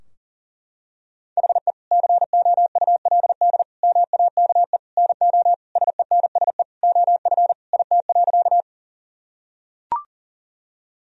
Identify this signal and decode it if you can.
Morse code, heard as a keyed tone, about 40 wpm, 700 Hz. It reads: HE COULD MAKE NO SENSE OF IT.
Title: Sentences from Top 1000 Words - Review Part 4 - 40wpm